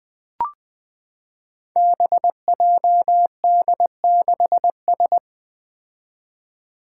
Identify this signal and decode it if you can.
Morse code, heard as a keyed tone, about 20 wpm, 700 Hz. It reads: BJD6S
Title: Five Challenging Letter-Number Combinations 20wpm